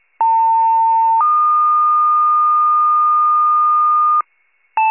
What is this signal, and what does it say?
Two-tone page, heard as a one-second tone, then a three-second tone, about 900 then 1220 Hz. It is heard from 0.2 to 4.2 s.